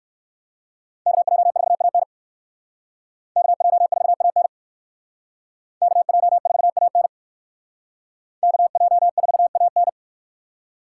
Morse code, heard as a keyed tone, about 60 wpm, 700 Hz. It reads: KJ4AN KJ4AN KJ4AN KJ4AN